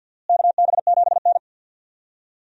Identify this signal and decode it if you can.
Morse code, heard as a keyed tone, about 46 wpm, 700 Hz. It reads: KB7N
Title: USA Call Signs - 4 Characters - Repeated 3x - 50wpm